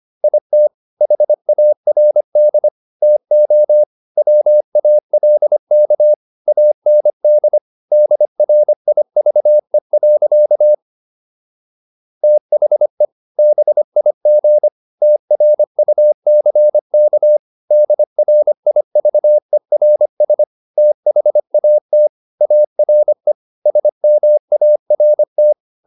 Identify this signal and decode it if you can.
Morse code, heard as a keyed tone, about 25 wpm, 600 Hz. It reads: IT HARD TO WALK AND DRIVE. THE BIG TRUCK DRIVERS THAT ARE SMART